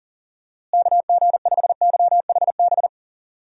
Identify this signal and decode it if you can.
Morse code, heard as a keyed tone, about 37 wpm, 700 Hz. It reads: KG5YHB
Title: USA Call Signs - 6 Characters - Repeated 2x - 40wpm